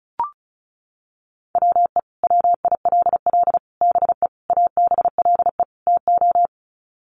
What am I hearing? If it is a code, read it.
Morse code, heard as a keyed tone, about 34 wpm, 700 Hz. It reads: WE WILL BE ABLE TO
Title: Sets of 5 Words Encore 35wpm